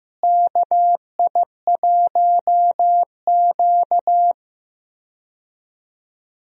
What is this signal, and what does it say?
Morse code, heard as a keyed tone, about 16 wpm, 700 Hz. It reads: KI1Q
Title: USA Call Signs - 4 Characters - Repeated 2x - 15wpm